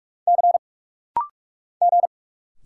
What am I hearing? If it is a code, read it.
Morse code, heard as a keyed tone, about 44 wpm, 700 Hz. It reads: C G
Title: Single Letters - Rapid-Fire 45wpm